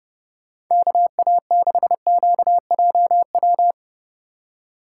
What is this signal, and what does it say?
Morse code, heard as a keyed tone, about 30 wpm, 700 Hz. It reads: KA6QJW
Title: USA Call Signs - 2x Character Spacing - 30wpm